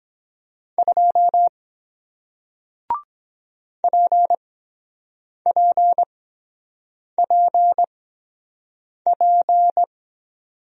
Morse code, heard as a keyed tone, about 23 wpm, spaced as for 7 wpm, 700 Hz. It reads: 2 PPPP